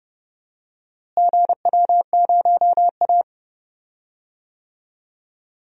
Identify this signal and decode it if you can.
Morse code, heard as a keyed tone, about 30 wpm, 700 Hz. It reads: GW0A
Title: DXCC-100 Call Signs - 4 Characters - Repeated 2x - 30wpm